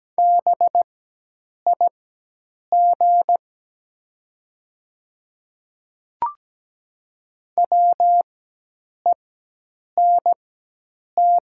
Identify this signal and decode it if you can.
Morse code, heard as a keyed tone, about 17 wpm, spaced as for 8 wpm, 690 Hz. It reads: BIG WENT
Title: 4x Character Spacing - Top 200 Words - 17wpm